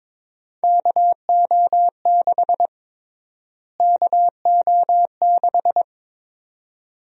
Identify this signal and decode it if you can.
Morse code, heard as a keyed tone, about 22 wpm, 700 Hz. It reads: KO6 KO6
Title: USA Call Sign Prefixes - Repeated 2x - 22wpm